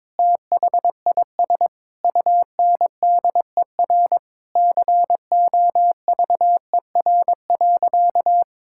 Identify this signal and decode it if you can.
Morse code, heard as a keyed tone, about 22 wpm, 700 Hz. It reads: THIS UNDER COVER.
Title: Sentences from Top 1000 Words - Review Part 6 - 22wpm